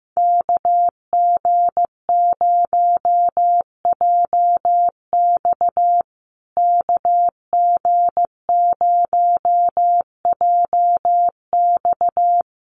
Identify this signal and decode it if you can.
Morse code, heard as a keyed tone, about 15 wpm, 700 Hz. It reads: KG0JX KG0JX
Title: USA Call Signs Repeated 3x 15wpm